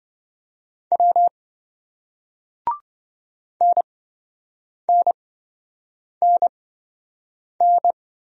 Morse code, heard as a keyed tone, about 26 wpm, spaced as for 7 wpm, 700 Hz. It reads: W NNNN